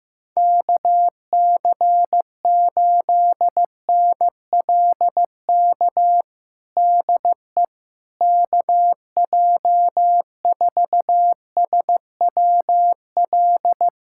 Morse code, heard as a keyed tone, about 15 wpm, 700 Hz. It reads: KC8NLK DE KJ4SWL
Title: USA Callsign DE Callsign 15wpm